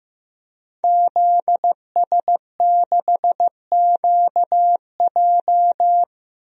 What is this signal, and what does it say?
Morse code, heard as a keyed tone, about 15 wpm, 700 Hz. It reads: ZS6QJ
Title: DXCC-100 Call Signs - 5 Characters - Repeated 2x - 15wpm